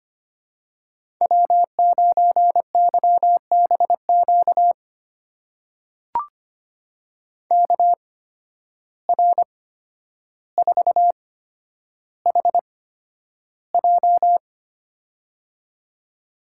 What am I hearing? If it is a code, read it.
Morse code, heard as a keyed tone, about 25 wpm, spaced as for 11 wpm, 700 Hz. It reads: W9YBQ KR4HJ
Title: USA Call Signs - 5 Characters - 8x Character Spacing - 25wpm